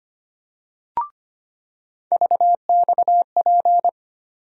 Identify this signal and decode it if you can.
Morse code, heard as a keyed tone, about 25 wpm, 700 Hz. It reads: VXP
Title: USA Call Sign Postfixes - 3 Letters 25wpm